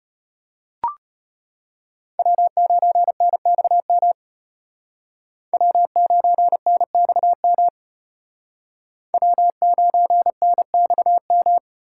Morse code, heard as a keyed tone, about 33 wpm, 700 Hz. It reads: W9NXM W9NXM W9NXM